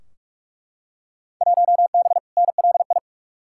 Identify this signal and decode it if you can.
Morse code, heard as a keyed tone, about 42 wpm, 700 Hz. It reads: 1B NLI
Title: Field Day Exchanges - 45wpm